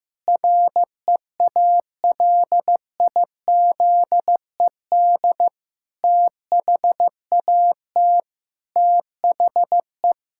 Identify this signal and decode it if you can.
Morse code, heard as a keyed tone, about 15 wpm, 700 Hz. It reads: REALIZED THAT THE
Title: Pangram Challenge v2 - 15wpm